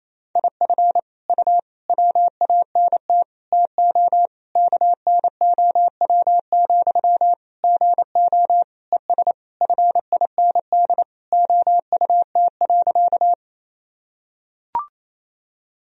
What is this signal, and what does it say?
Morse code, heard as a keyed tone, about 28 wpm, 700 Hz. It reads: IF U WANT TO KNOW, GO ES FIND OUT.
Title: Sentences from Top 200 Words and Abbreviations 28wpm